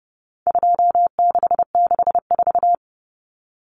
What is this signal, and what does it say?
Morse code, heard as a keyed tone, about 30 wpm, 700 Hz. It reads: 2664
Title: Common Cut Numbers - 4-Digit 30wpm